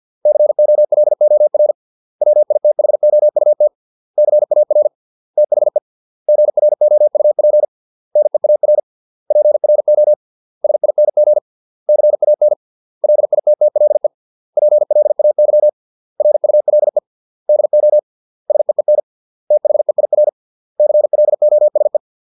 Morse code, heard as a keyed tone, about 50 wpm, 600 Hz. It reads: COLOR WITHOUT CAR THE GROUP NEAR PRO SING CAN LITTLE PLAY RULE DO SEEN THEIR CLOSE